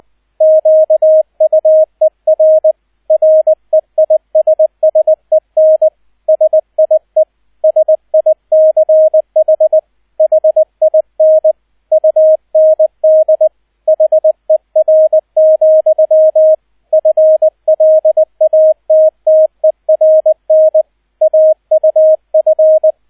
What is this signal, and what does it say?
Morse code, heard as a keyed tone, about 20 wpm, 630 Hz, with radio noise. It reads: QUER REISSEN SIE SICH HIN UND HER, FLATTERN AUF
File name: Erster Streich 20wpm